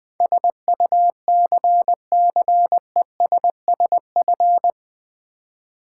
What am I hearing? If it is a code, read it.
Morse code, heard as a keyed tone, about 20 wpm, 700 Hz. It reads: SUCCESSF
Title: Build A Word - 4 Letters At A Time - 12-Letter Words - 20wpm